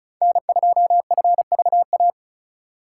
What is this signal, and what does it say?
Morse code, heard as a keyed tone, about 42 wpm, 700 Hz. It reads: N2FVA
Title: Speed Racing - USA 1x3 Call Signs 35wpm